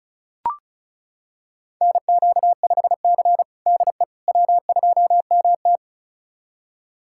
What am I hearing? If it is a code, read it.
Morse code, heard as a keyed tone, about 36 wpm, 700 Hz. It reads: NQ5C DE W2MT
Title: USA Callsign DE Callsign - 4 Characters - 35wpm